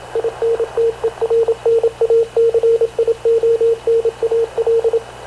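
Morse code, heard as a keyed tone, about 26 wpm, 460 Hz, with radio noise. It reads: INTERNACIONAL
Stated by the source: Havana Radio / CLA sending Spanish press (PX) in Morse, recorded by Radio Officer Paul Dumesnil onboard CNAV QUEST/CZDO.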